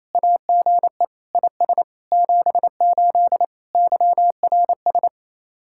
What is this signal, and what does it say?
Morse code, heard as a keyed tone, about 36 wpm, 700 Hz. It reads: AGE IS 78 YRS